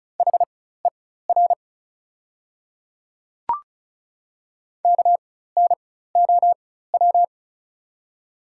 Morse code, heard as a keyed tone, about 36 wpm, spaced as for 16 wpm, 700 Hz. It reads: HER KNOW
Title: 4x Character Spacing - Top 300 Words - 35wpm